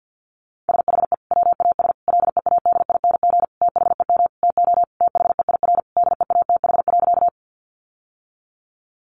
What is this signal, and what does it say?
Morse code, heard as a keyed tone, about 50 wpm, 700 Hz. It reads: SHE WAS LEADING THEM TO THEIR DEATH.
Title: Sentences from Top 1100 Words 50wpm